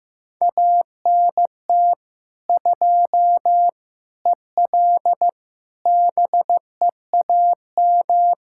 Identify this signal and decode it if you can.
Morse code, heard as a keyed tone, about 15 wpm, 700 Hz. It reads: ANT 2 EL BEAM
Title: Practice QSO - Exchange 2 - Info - 15wpm